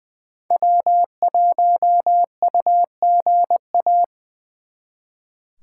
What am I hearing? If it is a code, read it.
Morse code, heard as a keyed tone, about 24 wpm, 700 Hz. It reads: W1UGA
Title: Speed Racing - USA Call Signs 20wpm